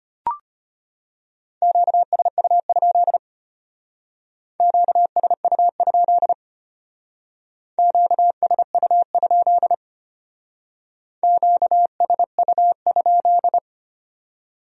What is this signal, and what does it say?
Morse code, heard as a keyed tone, about 32 wpm, 700 Hz. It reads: QSU? QSU? QSU? QSU?